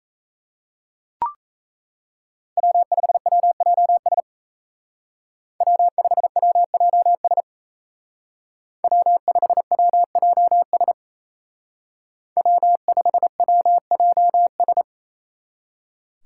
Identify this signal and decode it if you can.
Morse code, heard as a keyed tone, about 36 wpm, 700 Hz. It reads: W5WJS W5WJS W5WJS W5WJS